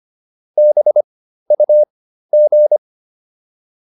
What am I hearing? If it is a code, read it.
Morse code, heard as a keyed tone, about 25 wpm, spaced as for 13 wpm, 600 Hz. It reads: BUG